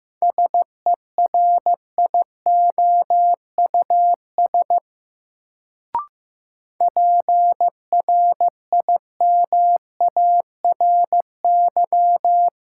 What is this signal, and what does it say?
Morse code, heard as a keyed tone, about 15 wpm, 700 Hz. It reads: SERIOUS PRIMARY
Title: Sentences from Top 1100 Words - New Word Drill Alt - 15wpm